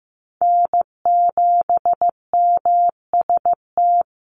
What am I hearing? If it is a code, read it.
Morse code, heard as a keyed tone, about 15 wpm, 700 Hz. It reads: N7MST
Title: USA Call Signs Repeated 2x 15wpm